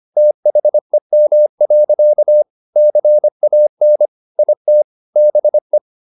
Morse code, heard as a keyed tone, about 25 wpm, 600 Hz. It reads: THEM. CAN IT BE
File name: the old man and the sea part 1 25wpm